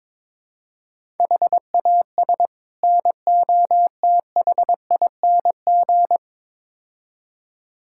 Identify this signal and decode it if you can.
Morse code, heard as a keyed tone, about 22 wpm, 700 Hz. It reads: HAS NOTHING
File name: practice / 22wpm / Build A Phrase - 5-Word Phrases - 22wpm